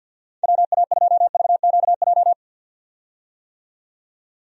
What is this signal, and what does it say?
Morse code, heard as a keyed tone, about 53 wpm, 700 Hz. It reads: WA2VQJ
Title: USA Call Signs - 6 Characters - 50wpm